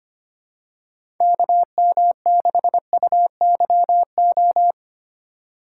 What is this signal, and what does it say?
Morse code, heard as a keyed tone, about 25 wpm, 700 Hz. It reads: KM6UYO
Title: USA Call Signs - 2x Character Spacing - 25wpm